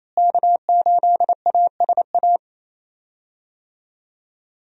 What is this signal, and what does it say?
Morse code, heard as a keyed tone, about 28 wpm, 700 Hz. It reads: K8ASA
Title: USA Call Signs - 5 Characters - Repeated 3x - 28wpm